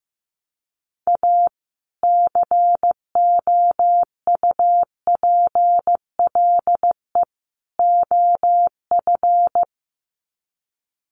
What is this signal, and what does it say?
Morse code, heard as a keyed tone, about 15 wpm, 700 Hz. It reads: A COUPLE OF
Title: Sets of 3 Words 15wpm